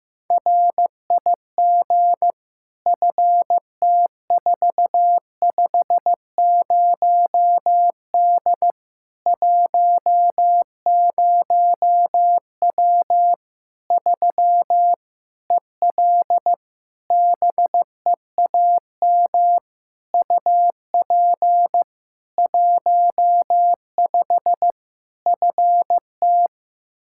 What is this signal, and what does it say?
Morse code, heard as a keyed tone, about 15 wpm, 700 Hz. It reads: RIG FT450D 10W 3 EL BEAM UP 15 FT
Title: Practice QSO - Exchange 2 - Info - 15wpm